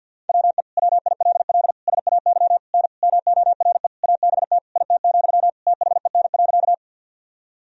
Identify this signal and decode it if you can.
Morse code, heard as a keyed tone, about 52 wpm, 700 Hz. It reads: WE WILL SAY N MORE ABT IT, THEN.
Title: Sentences from Top 200 Words and Abbreviations 50wpm